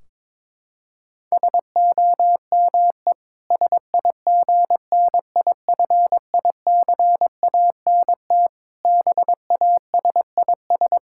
Morse code, heard as a keyed tone, about 22 wpm, 700 Hz. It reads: SOME SIGNIFICANT BASIS
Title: Sentences from Top 1600 Words 22wpm